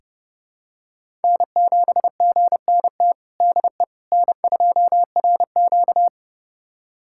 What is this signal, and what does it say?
Morse code, heard as a keyed tone, about 31 wpm, 700 Hz. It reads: N7GNT DE N2RQ
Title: USA Callsign DE Callsign 30wpm